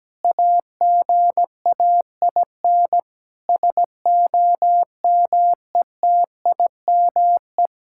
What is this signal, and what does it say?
Morse code, heard as a keyed tone, about 17 wpm, 700 Hz. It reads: AGAIN SOMETIME
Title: Sentences from Top 1000 Words - Review Part 6 - 17wpm